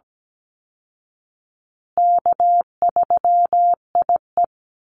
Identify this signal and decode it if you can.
Morse code, heard as a keyed tone, about 17 wpm, 700 Hz. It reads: K3IE
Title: Easy Call Signs Repeated 3x - 17wpm